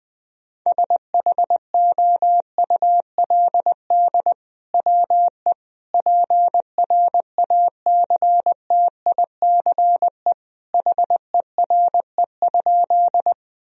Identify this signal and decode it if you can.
Morse code, heard as a keyed tone, about 20 wpm, 700 Hz. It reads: SHOULD WE PRACTICE HERE?
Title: Sentences from Top 1000 Words - Review Part 4 - 20wpm